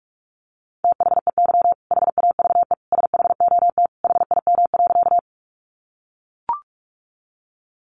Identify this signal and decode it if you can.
Morse code, heard as a keyed tone, about 44 wpm, 700 Hz. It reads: THEY HAVE SHOT HIM.
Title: Sentences from Top 1100 Words 45wpm